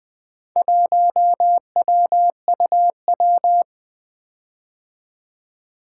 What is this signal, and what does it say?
Morse code, heard as a keyed tone, about 20 wpm, 700 Hz. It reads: 1WUW
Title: Four Letter-Number Combinations - 20wpm